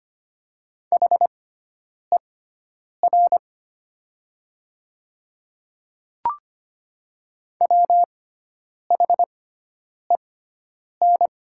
Morse code, heard as a keyed tone, about 25 wpm, spaced as for 9 wpm, 700 Hz. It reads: HER WHEN
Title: 6x Character Spacing - Top 100 Words - 25wpm